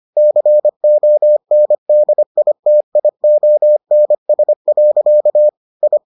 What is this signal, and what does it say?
Morse code, heard as a keyed tone, about 25 wpm, 600 Hz. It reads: CONDITIONS. I